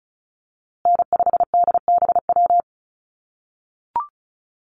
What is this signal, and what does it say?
Morse code, heard as a keyed tone, about 34 wpm, 700 Hz. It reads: N5DBW
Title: USA Call Signs Repeated 2x 35wpm